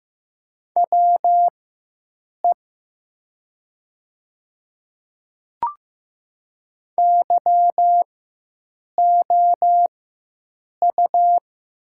Morse code, heard as a keyed tone, about 15 wpm, spaced as for 7 wpm, 700 Hz. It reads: WE YOU
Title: 4x Character Spacing - Top 100 Words - 15wpm